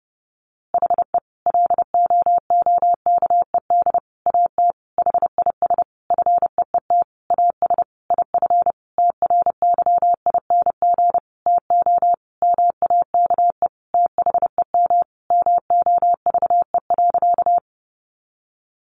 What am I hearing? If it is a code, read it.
Morse code, heard as a keyed tone, about 30 wpm, 700 Hz. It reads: HE LOOKED AT HIS FEET AS IF TRYING TO MAKE THEM MOVE.
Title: A Sound Of Thunder 30wpm